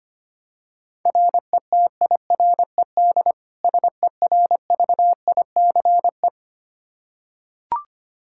Morse code, heard as a keyed tone, about 25 wpm, 700 Hz. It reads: RETIRED SERVICE
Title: Practice QSO - Exchange 3 - More Info - 25wpm